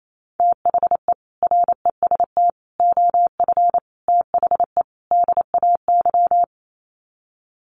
Morse code, heard as a keyed tone, about 28 wpm, 700 Hz. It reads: THE REST OF THE DAY
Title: Sets of 5 Words 28wpm